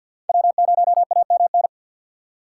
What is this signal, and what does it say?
Morse code, heard as a keyed tone, about 52 wpm, 700 Hz. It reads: W0AMN
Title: USA Call Signs - 5 Characters - 4x Character Spacing - 50wpm